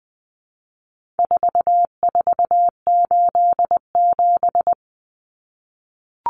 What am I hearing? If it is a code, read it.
Morse code, heard as a keyed tone, about 20 wpm, 700 Hz. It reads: 4487
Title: Numbers - 4-Digit 20wpm